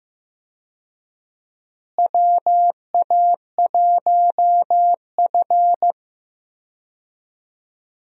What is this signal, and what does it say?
Morse code, heard as a keyed tone, about 15 wpm, 700 Hz. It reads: WA1F
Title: USA Call Signs - 4 Characters - Repeated 2x - 15wpm